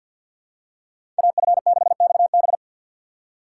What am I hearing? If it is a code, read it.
Morse code, heard as a keyed tone, about 53 wpm, 690 Hz. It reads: AF6XB
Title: USA Call Signs - 6x Character Spacing - 50wpm